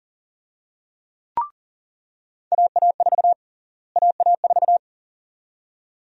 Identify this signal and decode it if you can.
Morse code, heard as a keyed tone, about 38 wpm, 700 Hz. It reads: AA4 AA4
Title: USA Call Sign Prefixes - Repeated 2x - 40wpm